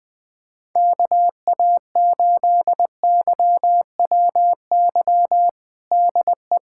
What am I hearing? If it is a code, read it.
Morse code, heard as a keyed tone, about 21 wpm, 700 Hz. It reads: KA8YWY DE
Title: USA Callsign DE Callsign - 6 Characters - 20wpm